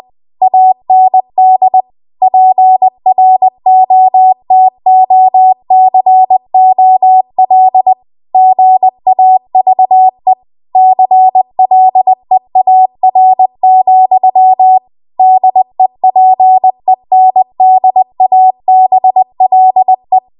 Morse code, heard as a keyed tone, about 20 wpm, 750 Hz. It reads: AND PROTOCOL GAVE CLEAR, DEPENDABLE